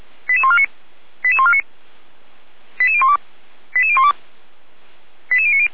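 Five-tone selective call (ZVEI), heard as quick runs of five tones, tones of about 80 ms. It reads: 88149 88149 88012 88012 88090
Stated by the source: ZVEI five-tone call received by SDR, from feuerwehr-orlamuende.org fire brigade archive.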